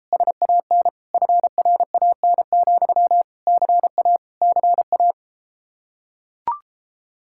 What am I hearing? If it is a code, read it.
Morse code, heard as a keyed tone, about 33 wpm, 700 Hz. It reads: SAN FRAN, CA CA